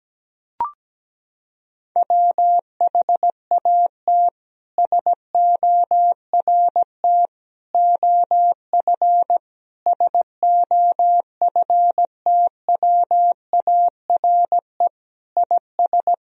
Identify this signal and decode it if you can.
Morse code, heard as a keyed tone, about 17 wpm, 700 Hz. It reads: WHAT SORT OF SOFTWARE IS
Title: Sentences from Top 1000 Words - Review Part 3 - 17wpm